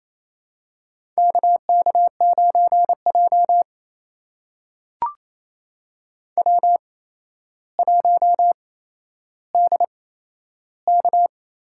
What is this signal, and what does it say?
Morse code, heard as a keyed tone, about 29 wpm, spaced as for 12 wpm, 700 Hz. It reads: KK9J W1DK
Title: USA Call Signs - 4 Characters - 8x Character Spacing - 28wpm